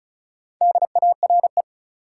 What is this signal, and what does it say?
Morse code, heard as a keyed tone, about 36 wpm, 700 Hz. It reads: DARE
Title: Four Letter Words - 35wpm